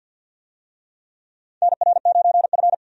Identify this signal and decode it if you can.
Morse code, heard as a keyed tone, about 49 wpm, 700 Hz. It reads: NR9F
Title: USA Call Signs - 4 Characters - 50wpm